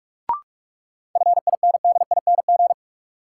Morse code, heard as a keyed tone, about 46 wpm, 700 Hz. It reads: FINDING
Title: Sentences from Top 1100 Words - New Word Drill Alt - 45wpm